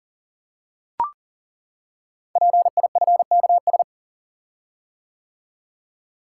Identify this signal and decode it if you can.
Morse code, heard as a keyed tone, about 39 wpm, 690 Hz. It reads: PIFKS
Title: Code Groups - 5 Letters - 40wpm